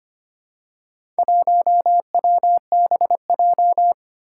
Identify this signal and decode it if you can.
Morse code, heard as a keyed tone, about 25 wpm, 700 Hz. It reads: 1WBJ
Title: Four Challenging Letter-Number Combinations 25wpm